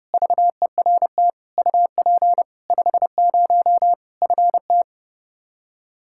Morse code, heard as a keyed tone, about 32 wpm, 700 Hz. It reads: VERT UP 50 FT
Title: Practice QSO - Exchange 2 - Info - 30wpm